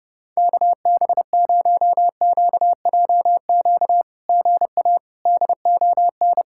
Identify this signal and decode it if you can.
Morse code, heard as a keyed tone, about 30 wpm, 700 Hz. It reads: KB0QJQ GA DON
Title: SOTA QSO Practice 30wpm